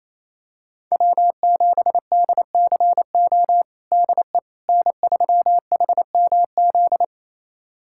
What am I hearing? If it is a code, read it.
Morse code, heard as a keyed tone, about 28 wpm, 700 Hz. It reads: W7DCO DE N3HMZ
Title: USA Callsign DE Callsign - 5 Characters - 28wpm